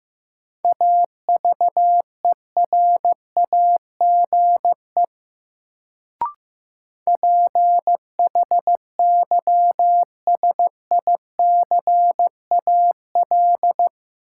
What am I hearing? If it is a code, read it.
Morse code, heard as a keyed tone, about 16 wpm, 700 Hz. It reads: AVERAGE PHYSICAL
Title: Sentences from Top 1400 Words - New Word Drill Alt - 15wpm